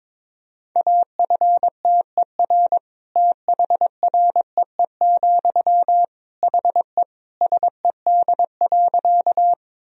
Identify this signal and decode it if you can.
Morse code, heard as a keyed tone, about 22 wpm, 700 Hz. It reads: AFTER THREE, HE SED.
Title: Sentences from Top 200 Words and Abbreviations 22wpm